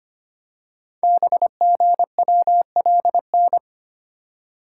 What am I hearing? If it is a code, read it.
Morse code, heard as a keyed tone, about 25 wpm, 700 Hz. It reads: BGWLN
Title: Code Groups - 5 Letters - 25wpm